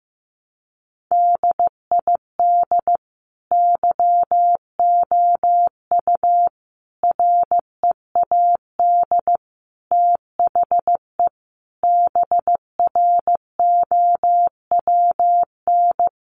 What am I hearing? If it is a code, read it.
Morse code, heard as a keyed tone, about 15 wpm, 700 Hz. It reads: DID YOU READ THE BROWN
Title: Sentences from Top 1200 Words 15wpm